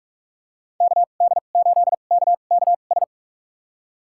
Morse code, heard as a keyed tone, about 46 wpm, spaced as for 32 wpm, 700 Hz. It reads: KD8KKS
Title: USA Call Signs - 2x Character Spacing - 45wpm